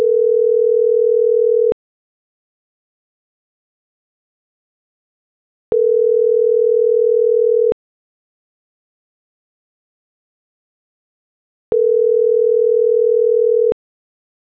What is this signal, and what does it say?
Ringback tone (North America), 440 + 480 Hz.